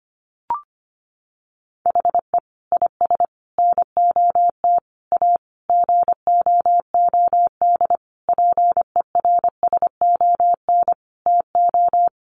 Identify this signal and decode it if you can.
Morse code, heard as a keyed tone, about 25 wpm, 700 Hz. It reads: HE IS NOT A GOOD PERSON TO
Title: Sentences from Top 1400 Words 25wpm